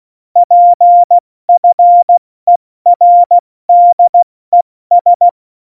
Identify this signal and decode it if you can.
Morse code, heard as a keyed tone, about 16 wpm, 700 Hz. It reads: PFERDES
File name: Morsekurs / Witze / Witz 01. Dezember 2020, 70 F80